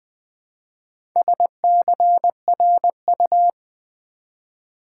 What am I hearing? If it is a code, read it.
Morse code, heard as a keyed tone, about 20 wpm, 700 Hz. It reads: SCRU
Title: Build A Word - 2 Letters At A Time - 8-Letter Words - 20wpm